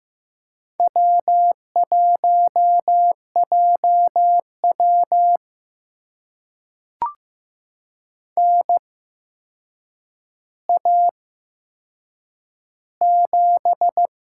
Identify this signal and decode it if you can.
Morse code, heard as a keyed tone, about 15 wpm, spaced as for 7 wpm, 700 Hz. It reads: W1JW NA7
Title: USA Call Signs - 4 Characters - 8x Character Spacing - 15wpm